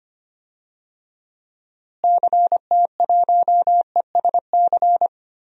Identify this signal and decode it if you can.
Morse code, heard as a keyed tone, about 25 wpm, 700 Hz. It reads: CT1ESC
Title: DXCC-100 Call Signs - 6 Characters - Repeated 2x - 25wpm